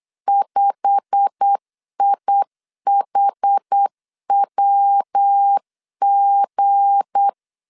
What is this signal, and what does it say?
Morse code, heard as a keyed tone, about 8 wpm, 800 Hz. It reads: 5IHWG
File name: LEZIONI / 8 YJ45 / 8 YJ45 8wpm 3214